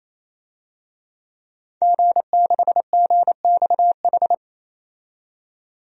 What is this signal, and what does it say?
Morse code, heard as a keyed tone, about 28 wpm, 700 Hz. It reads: G6GXH
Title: DXCC-100 Call Signs - 5 Characters - 28wpm